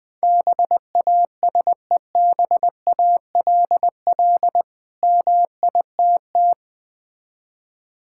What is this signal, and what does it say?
Morse code, heard as a keyed tone, about 20 wpm, 700 Hz. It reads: BASEBALL MITT
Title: Difficult Words with Hints - 20wpm